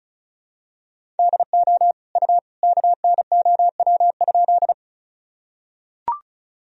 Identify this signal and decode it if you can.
Morse code, heard as a keyed tone, about 36 wpm, 700 Hz. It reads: DO U KNOW?
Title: Sentences from Top 100 Words and Abbreviations 35wpm